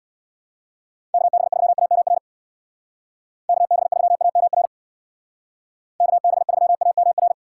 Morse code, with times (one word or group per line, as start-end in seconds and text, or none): 1.1-2.2 KB3AMR
3.5-4.7 KB3AMR
6.0-7.3 KB3AMR